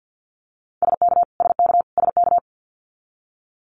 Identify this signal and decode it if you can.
Morse code, heard as a keyed tone, about 50 wpm, 700 Hz. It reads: SK SK SK
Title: Canadian Provinces and Territories - Repeated 3x - 50wpm